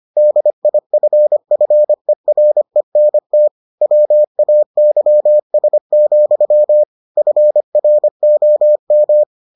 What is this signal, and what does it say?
Morse code, heard as a keyed tone, about 25 wpm, 600 Hz. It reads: DIFFERENT WAYS, FROM